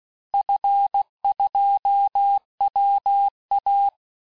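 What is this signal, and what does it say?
Morse code, heard as a keyed tone, about 16 wpm, 800 Hz. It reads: F2WA
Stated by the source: Morse course text, end of QSO, sent at 15 words per minute.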